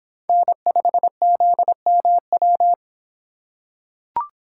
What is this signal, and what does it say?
Morse code, heard as a keyed tone, about 25 wpm, 700 Hz. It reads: N5ZMW